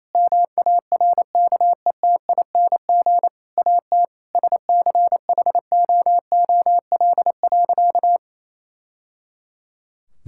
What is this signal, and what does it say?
Morse code, heard as a keyed tone, about 28 wpm, 700 Hz. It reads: MARKETING AT SCHOOL.
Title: Sentences from Top 1000 Words - Review Part 5 - 28wpm